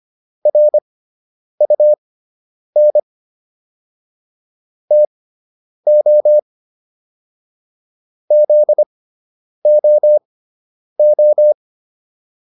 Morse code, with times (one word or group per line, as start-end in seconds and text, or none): 0.5-3.0 RUN
4.9-6.4 TO
8.3-11.5 ZOO